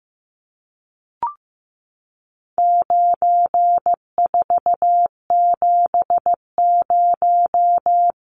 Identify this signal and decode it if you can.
Morse code, heard as a keyed tone, about 15 wpm, 700 Hz. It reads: 9470
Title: Numbers - 4-Digit 15wpm